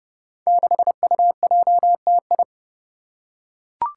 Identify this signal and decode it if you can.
Morse code, heard as a keyed tone, about 30 wpm, 700 Hz. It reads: 6UJTI
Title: Five Challenging Letter-Number Combinations 30wpm